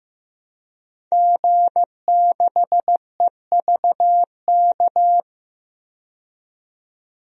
Morse code, heard as a keyed tone, about 15 wpm, 700 Hz. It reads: G6EVK
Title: DXCC-100 Call Signs - 5 Characters - 15wpm